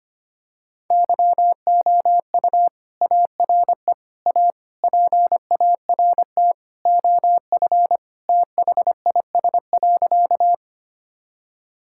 Morse code, heard as a keyed tone, about 25 wpm, 700 Hz. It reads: YOU ARE A PART OF THIS.
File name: practice / 25wpm / Sentences with 3-Word Phrases - 25wpm